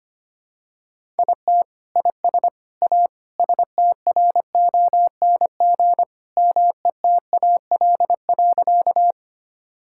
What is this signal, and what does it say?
Morse code, heard as a keyed tone, about 25 wpm, 700 Hz. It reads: IT IS A STRONG METAL.
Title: Sentences from Top 1000 Words - Review Part 6 - 25wpm